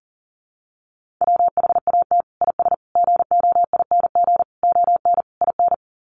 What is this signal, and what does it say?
Morse code, heard as a keyed tone, about 40 wpm, 700 Hz. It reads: WHAT IS GOING ON IN
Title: Sets of 5 Words Encore2 40wpm